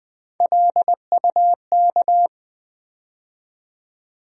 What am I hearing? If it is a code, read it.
Morse code, heard as a keyed tone, about 20 wpm, 700 Hz. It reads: LUK
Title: USA Call Sign Postfixes - 3 Letters 20wpm